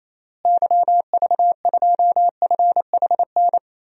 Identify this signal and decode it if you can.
Morse code, heard as a keyed tone, about 28 wpm, 700 Hz. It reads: YV2FHN
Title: DXCC-100 Call Signs - 6 Characters - Repeated 3x - 28wpm